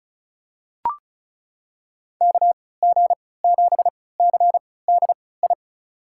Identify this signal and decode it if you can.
Morse code, heard as a keyed tone, about 36 wpm, spaced as for 20 wpm, 700 Hz. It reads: KG7CDI
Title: USA Call Signs - 6 Characters - 3x Character Spacing - 35wpm